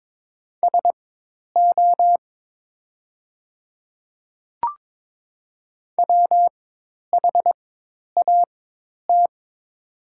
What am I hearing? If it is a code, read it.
Morse code, heard as a keyed tone, about 22 wpm, spaced as for 10 wpm, 690 Hz. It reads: SO WHAT
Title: 4x Character Spacing - Top 500 Words - 22wpm